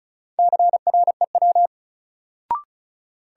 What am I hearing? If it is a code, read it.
Morse code, heard as a keyed tone, about 34 wpm, 700 Hz. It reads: CREW
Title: Sentences from Top 1500 Words - New Word Drill Alt - 35wpm